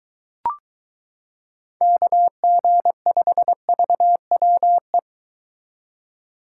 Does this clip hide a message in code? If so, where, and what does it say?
Yes: Morse code from 1.8 to 5.0 s, heard as a keyed tone, about 23 wpm, 700 Hz. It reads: KG5VWE